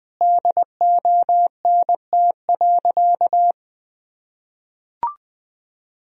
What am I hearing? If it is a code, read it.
Morse code, heard as a keyed tone, about 20 wpm, 700 Hz. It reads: DONT.
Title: Sentences from Top 1000 Words - Review Part 4 - 20wpm